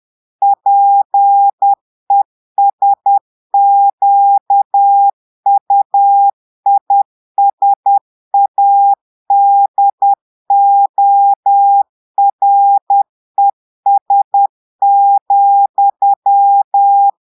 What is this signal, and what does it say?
Morse code, heard as a keyed tone, about 10 wpm, 800 Hz. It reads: PESQUISADORES,
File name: Portugais - texte en clair 1 - vitesse - 10 mots minute